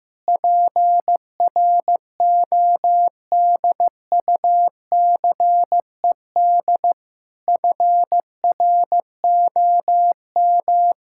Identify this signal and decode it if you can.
Morse code, heard as a keyed tone, about 15 wpm, 700 Hz. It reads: PRODUCED FROM
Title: Sentences from Top 1000 Words - Review Part 6 - 15wpm